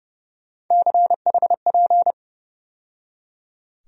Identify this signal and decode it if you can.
Morse code, heard as a keyed tone, about 30 wpm, 700 Hz. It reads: CHP
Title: USA Call Sign Postfixes - 3 Letters 30wpm